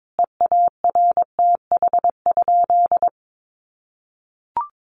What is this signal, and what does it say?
Morse code, heard as a keyed tone, about 22 wpm, 700 Hz. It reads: EARTH?
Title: Sentences from Top 1500 Words 22wpm